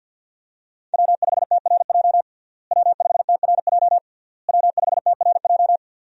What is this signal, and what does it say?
Morse code, heard as a keyed tone, about 47 wpm, 700 Hz. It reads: W5TRJ W5TRJ W5TRJ
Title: USA Call Signs - 5 Characters - Repeated 3x - 50wpm